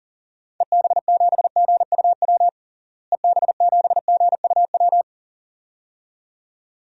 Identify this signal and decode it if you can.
Morse code, heard as a keyed tone, about 40 wpm, 700 Hz. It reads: EB7GUW EB7GUW
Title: DXCC-100 Call Signs - 6 Characters - Repeated 2x - 40wpm